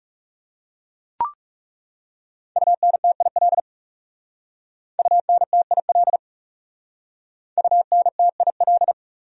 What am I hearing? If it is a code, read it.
Morse code, heard as a keyed tone, about 37 wpm, 700 Hz. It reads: UNTIL UNTIL UNTIL
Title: Speed-Racing Top 500 Words 30wpm